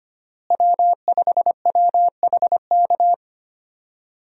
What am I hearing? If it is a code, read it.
Morse code, heard as a keyed tone, about 25 wpm, 700 Hz. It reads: W5WHK
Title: USA Call Signs - 5 Characters - Repeated 3x - 25wpm